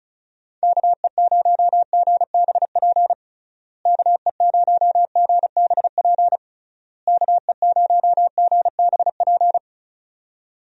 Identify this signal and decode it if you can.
Morse code, heard as a keyed tone, about 36 wpm, 700 Hz. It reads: KE0GBP KE0GBP KE0GBP
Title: USA Call Signs - 6 Characters - Repeated 3x - 35wpm